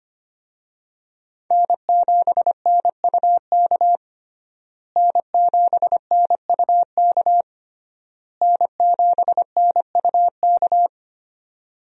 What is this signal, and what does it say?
Morse code, heard as a keyed tone, about 25 wpm, 700 Hz. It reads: N7NUK N7NUK N7NUK
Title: USA Call Signs - 5 Characters - Repeated 3x - 25wpm